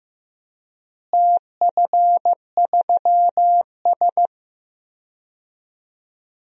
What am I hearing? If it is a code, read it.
Morse code, heard as a keyed tone, about 15 wpm, 700 Hz. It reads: TF3S
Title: DXCC-100 Call Signs - 4 Characters - Repeated 2x - 15wpm